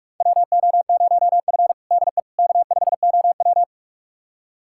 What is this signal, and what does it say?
Morse code, heard as a keyed tone, about 46 wpm, 700 Hz. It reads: WO0F DE K5OW
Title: USA Callsign DE Callsign - 4 Characters - 45wpm